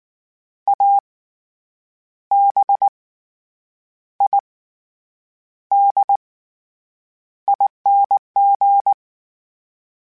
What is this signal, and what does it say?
Morse code, heard as a keyed tone, about 19 wpm, spaced as for 8 wpm, 800 Hz. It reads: ABIDING
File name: Special INGs / Special INGs-mp3s / SlowToFaster ing @18